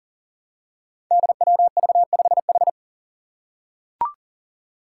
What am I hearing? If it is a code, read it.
Morse code, heard as a keyed tone, about 40 wpm, 690 Hz. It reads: DWV5H
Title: Five Challenging Letter-Number Combinations 40wpm